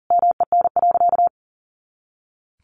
Morse code, heard as a keyed tone, about 40 wpm, 700 Hz. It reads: MEN.
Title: Sentences from Top 300 Words 40wpm